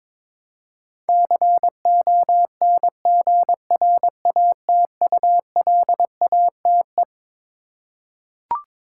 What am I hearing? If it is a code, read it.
Morse code, heard as a keyed tone, about 22 wpm, 700 Hz. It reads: CONGRATULATE
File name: Twelve Letter Words - 22wpm